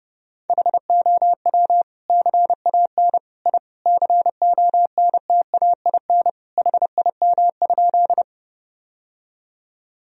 Morse code, heard as a keyed tone, about 32 wpm, 700 Hz. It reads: HOW CAN I CONTAIN HIM?
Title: Sentences from Top 1000 Words - Review Part 2 - 30wpm